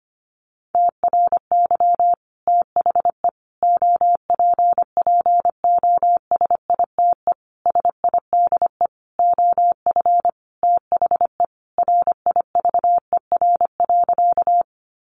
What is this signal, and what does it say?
Morse code, heard as a keyed tone, about 25 wpm, 700 Hz. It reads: TRY THE OPPOSITE SIDE OF THE RIVER.
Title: Sentences from Top 1300 Words 25wpm